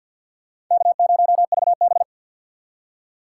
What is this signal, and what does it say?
Morse code, heard as a keyed tone, about 46 wpm, 700 Hz. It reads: K0VB
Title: USA Call Signs - 2x Character Spacing - 50wpm